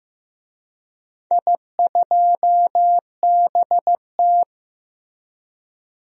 Morse code, heard as a keyed tone, about 15 wpm, 700 Hz. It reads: I2BT
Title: DXCC-100 Call Signs - 4 Characters - Repeated 3x - 15wpm